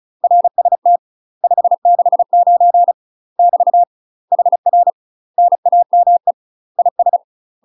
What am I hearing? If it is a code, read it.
Morse code, heard as a keyed tone, about 34 wpm, 700 Hz. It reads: RST 569 <BT> HR NAME IS
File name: UK MASTER QSOs TRAINING / 35WPM / UK MASTER QSOs 0005